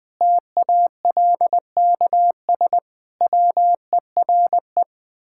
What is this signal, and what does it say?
Morse code, heard as a keyed tone, about 20 wpm, 700 Hz. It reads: TALKS WERE
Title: Sentences with 8-Word Phrases - Spoken Beginning - 20wpm